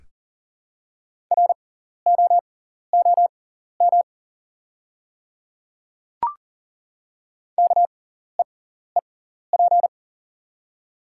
Morse code, heard as a keyed tone, about 40 wpm, spaced as for 14 wpm, 700 Hz. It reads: ROOM KEEP
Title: 6x Character Spacing - Top 300 Words - 40wpm